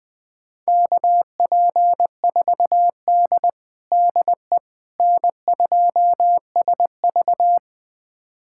Morse code, heard as a keyed tone, about 20 wpm, 700 Hz. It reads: KP4D DE N2SV
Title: USA Callsign DE Callsign - 4 Characters - 20wpm